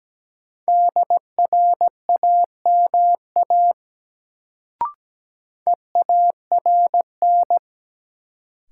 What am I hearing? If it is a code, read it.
Morse code, heard as a keyed tone, about 17 wpm, 700 Hz. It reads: DRAMA EARN
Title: Sentences from Top 1500 Words - New Word Drill - 17wpm